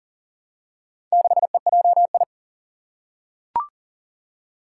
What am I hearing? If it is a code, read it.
Morse code, heard as a keyed tone, about 40 wpm, 700 Hz. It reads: 6EJI
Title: Four Challenging Letter-Number Combinations 40wpm